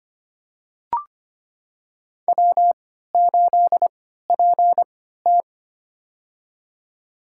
Morse code, heard as a keyed tone, about 25 wpm, spaced as for 14 wpm, 700 Hz. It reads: W8PT
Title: USA Call Signs - 4 Characters - 3x Character Spacing - 25wpm